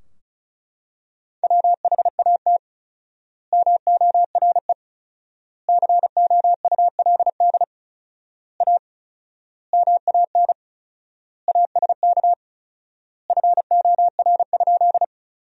Morse code, heard as a keyed tone, about 36 wpm, 700 Hz. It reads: WHAT MORE COULD A MAN ASK FOR?
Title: 4x Word-Spacing - Sentences from Top 200 Words - 35wpm